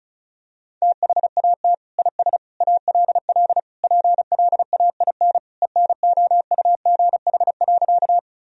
Morse code, heard as a keyed tone, about 34 wpm, 700 Hz. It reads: THAT IS ALL PLAIN ENOUGH.
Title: Sentences from Top 1000 Words - Review Part 2 - 35wpm